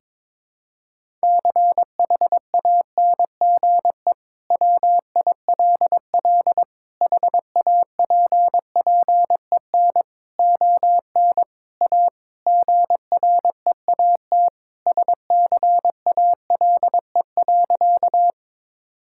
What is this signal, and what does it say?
Morse code, heard as a keyed tone, about 21 wpm, 700 Hz. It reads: CHANGE WILL HAPPEN ON A GREAT SCALE.
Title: Sentences from Top 1000 Words - Review Part 5 - 22wpm